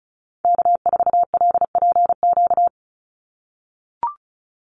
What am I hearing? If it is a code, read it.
Morse code, heard as a keyed tone, about 34 wpm, 700 Hz. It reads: K4LPQ
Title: NA SOTA Activator Callsigns 35wpm